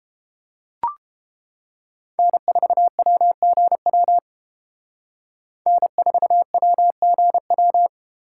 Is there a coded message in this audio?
Yes: Morse code from 2.2 to 7.9 s, heard as a keyed tone, about 28 wpm, 700 Hz. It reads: N4WGW N4WGW